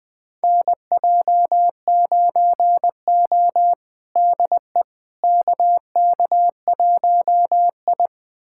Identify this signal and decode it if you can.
Morse code, heard as a keyed tone, about 21 wpm, 700 Hz. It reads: NJ9O DE KK1I
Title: USA Callsign DE Callsign - 4 Characters - 20wpm